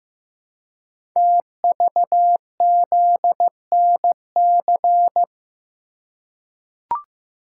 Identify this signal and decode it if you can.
Morse code, heard as a keyed tone, about 15 wpm, 700 Hz. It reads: TVZNC
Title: Code Groups - 5 Letter:Numbers - 15wpm